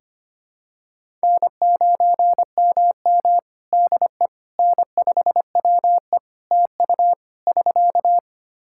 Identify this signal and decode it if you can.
Morse code, heard as a keyed tone, about 25 wpm, 700 Hz. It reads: N9MM DE N5WE TU <SK>
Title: Practice QSO - Exchange 4 - Ending - 25wpm